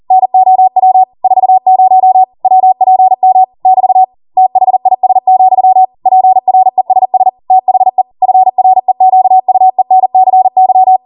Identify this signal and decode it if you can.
Morse code, heard as a keyed tone, about 40 wpm, 750 Hz. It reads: NOW 40 WPM <BT> THIS, PRESS THE FREQUENCY